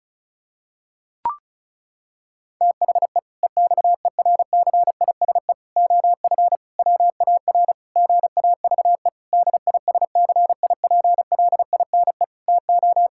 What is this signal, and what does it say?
Morse code, heard as a keyed tone, about 34 wpm, 700 Hz. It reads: THE EXERCISE OF WAR GAVE DISCIPLINE TO
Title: Sentences from Top 1000 Words - Review Part 6 - 35wpm